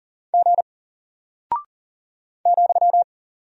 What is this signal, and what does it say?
Morse code, heard as a keyed tone, about 40 wpm, 700 Hz. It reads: G ,
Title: Single Character (Letter:Number:Punctuation:Prosign) Alt 40wpm